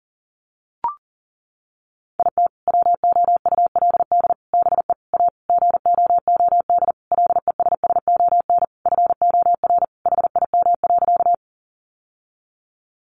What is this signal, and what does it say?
Morse code, heard as a keyed tone, about 40 wpm, 700 Hz. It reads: IT WOULD BE A GOOD LESSON FOR HIM.
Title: Sentences from Top 1200 Words 40wpm